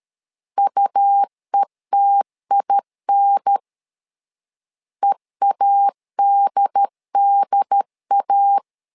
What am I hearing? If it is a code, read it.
Morse code, heard as a keyed tone, about 13 wpm, 800 Hz. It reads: UETIN EADDA
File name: LEZIONI / 3 MWRO / 3 M 12wpm 1541